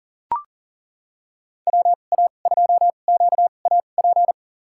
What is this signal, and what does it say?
Morse code, heard as a keyed tone, about 40 wpm, spaced as for 29 wpm, 700 Hz. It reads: WA2QAP
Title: USA Call Signs - 6 Characters - 2x Character Spacing - 40wpm